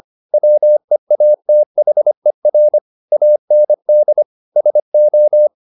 Morse code, heard as a keyed tone, about 25 wpm, 600 Hz. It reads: WEATHER AND SO
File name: the old man and the sea part 1 25wpm